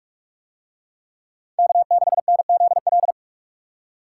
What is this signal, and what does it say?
Morse code, heard as a keyed tone, about 44 wpm, 700 Hz. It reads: K6NZL